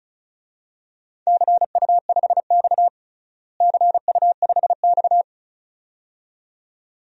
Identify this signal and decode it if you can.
Morse code, heard as a keyed tone, about 36 wpm, 700 Hz. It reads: CU5X CU5X
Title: DXCC-100 Call Signs - 4 Characters - Repeated 2x - 35wpm